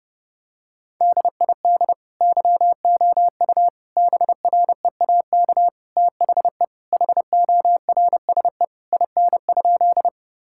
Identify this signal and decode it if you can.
Morse code, heard as a keyed tone, about 32 wpm, 700 Hz. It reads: DID YOU BREAK THE HORSE IN?
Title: Sentences from Top 1000 Words - Review Part 6 - 30wpm